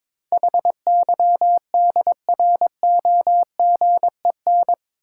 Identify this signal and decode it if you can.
Morse code, heard as a keyed tone, about 22 wpm, 700 Hz. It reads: HYDROGEN
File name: practice / 22wpm / Eight Letter Words - 22wpm